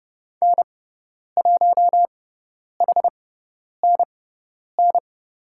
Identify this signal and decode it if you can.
Morse code, heard as a keyed tone, about 30 wpm, spaced as for 10 wpm, 700 Hz. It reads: N1HNN
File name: Lesson 17g - Letter u - Call Signs 30wpm